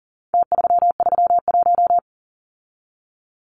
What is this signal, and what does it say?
Morse code, heard as a keyed tone, about 40 wpm, 700 Hz. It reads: T331
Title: Numbers - Contest Style - 4-Digit 40wpm